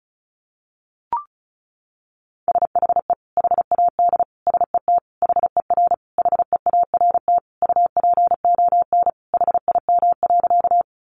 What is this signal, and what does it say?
Morse code, heard as a keyed tone, about 34 wpm, 700 Hz. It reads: SHE HAD SET HER HEART UPON HIM.
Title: Sentences from Top 1200 Words 35wpm